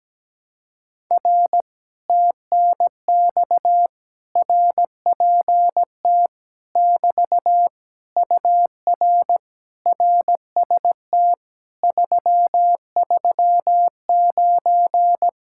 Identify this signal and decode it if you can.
Morse code, heard as a keyed tone, about 17 wpm, 700 Hz. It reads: R TNX RPT <BT> UR RST 339
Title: Speed-Racing Common QSO Phrases - 17wpm